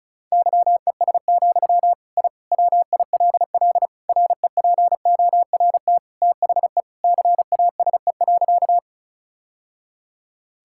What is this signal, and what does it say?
Morse code, heard as a keyed tone, about 34 wpm, 700 Hz. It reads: YES, I WILL REPORT THE CASE.
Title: Sentences from Top 1000 Words - Review Part 6 - 35wpm